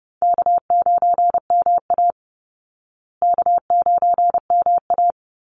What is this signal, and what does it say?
Morse code, heard as a keyed tone, about 30 wpm, 700 Hz. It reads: K9MA K9MA
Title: Easy Call Signs Repeated 2x - 30wpm